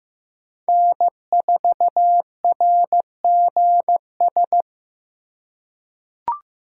Morse code, heard as a keyed tone, about 15 wpm, 700 Hz. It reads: N4RGS
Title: USA Call Signs - 3x Character Spacing - 15wpm